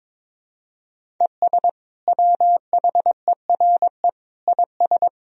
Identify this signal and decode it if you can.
Morse code, heard as a keyed tone, about 22 wpm, 700 Hz. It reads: ES WHERE IS
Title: Sentences from Top 200 Words and Abbreviations 22wpm